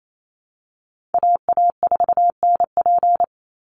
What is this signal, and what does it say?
Morse code, heard as a keyed tone, about 28 wpm, 700 Hz. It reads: AA4NP
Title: Call Signs Repeated 2x - 28wpm